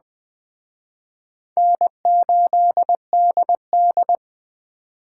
Morse code, heard as a keyed tone, about 20 wpm, 700 Hz. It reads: N8DD
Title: Speed Racing - USA Call Signs 17wpm